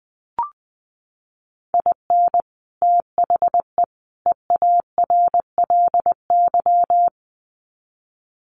Morse code, heard as a keyed tone, about 20 wpm, 700 Hz. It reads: IN THE EARLY
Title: Sets of 3 Words 20wpm